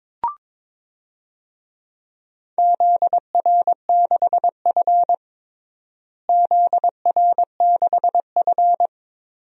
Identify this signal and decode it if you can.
Morse code, heard as a keyed tone, about 22 wpm, 700 Hz. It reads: ZR6F ZR6F
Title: DXCC-100 Call Signs - 4 Characters - Repeated 3x - 22wpm